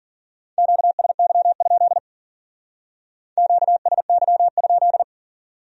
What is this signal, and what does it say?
Morse code, heard as a keyed tone, about 52 wpm, 700 Hz. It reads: QSY? QSY?